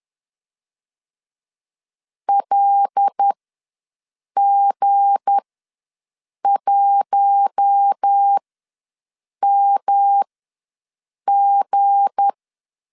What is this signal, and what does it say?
Morse code, heard as a keyed tone, about 10 wpm, spaced as for 6 wpm, 800 Hz. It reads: LG1MG
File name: LEZIONI / 7 GF9Z / 7 G ws 10wpm 3843